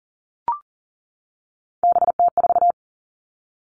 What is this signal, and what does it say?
Morse code, heard as a keyed tone, about 41 wpm, 690 Hz. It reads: BT4
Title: Three Challenging Letter-Number Combinations 40wpm